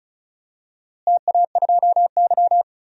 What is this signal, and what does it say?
Morse code, heard as a keyed tone, about 34 wpm, 700 Hz. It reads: TA2Y
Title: DXCC-100 Call Signs - 4 Characters - Repeated 2x - 35wpm